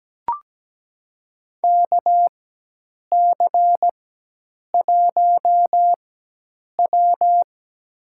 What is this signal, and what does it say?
Morse code, heard as a keyed tone, about 17 wpm, spaced as for 8 wpm, 700 Hz. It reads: KC1W
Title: USA Call Signs - 4 Characters - 4x Character Spacing - 17wpm